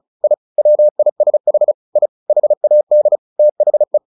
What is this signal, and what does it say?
Morse code, heard as a keyed tone, about 35 wpm, 600 Hz. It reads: I WISH I HAD THE
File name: the old man and the sea / the old man and the sea 1 35wpm